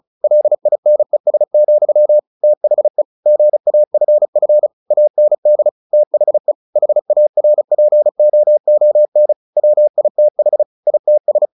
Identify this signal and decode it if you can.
Morse code, heard as a keyed tone, about 34 wpm, 600 Hz. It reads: LINES, THE GAFF AND THE HARPOON WITH ITS
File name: the old man and the sea / the old man and the sea 1 35wpm